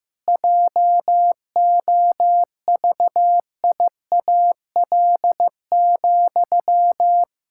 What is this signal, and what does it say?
Morse code, heard as a keyed tone, about 15 wpm, 700 Hz. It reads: JOVIAL,
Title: Pangram Challenge v2 - 15wpm